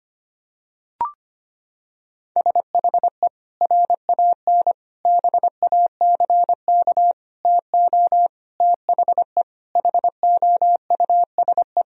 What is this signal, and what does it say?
Morse code, heard as a keyed tone, about 25 wpm, 700 Hz. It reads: SHE RAN BACK TO THE HOUSE
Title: Sentences from Top 1000 Words - Review Part 3 - 25wpm